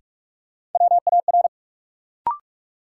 Morse code, heard as a keyed tone, about 46 wpm, 700 Hz. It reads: WAR
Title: Sets of 5 Words - New Word Drill - 45wpm